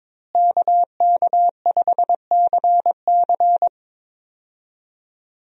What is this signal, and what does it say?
Morse code, heard as a keyed tone, about 22 wpm, 700 Hz. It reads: KK5CC
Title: USA Call Signs - 5 Characters - Repeated 3x - 22wpm